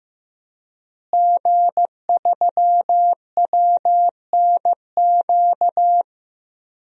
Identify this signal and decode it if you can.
Morse code, heard as a keyed tone, about 16 wpm, 700 Hz. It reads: G3WNQ
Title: DXCC-100 Call Signs - 5 Characters - Repeated 3x - 15wpm